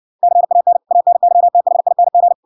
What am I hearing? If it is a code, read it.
Morse code, heard as a keyed tone, about 66 wpm, 700 Hz. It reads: CAN ANYTHING